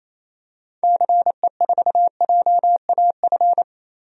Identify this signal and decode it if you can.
Morse code, heard as a keyed tone, about 28 wpm, 700 Hz. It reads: CE4JAF
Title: DXCC-100 Call Signs - 6 Characters - 28wpm